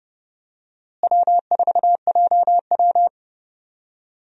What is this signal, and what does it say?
Morse code, heard as a keyed tone, about 31 wpm, 700 Hz. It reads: W4JW
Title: USA Call Signs - 4 Characters - Repeated 3x - 30wpm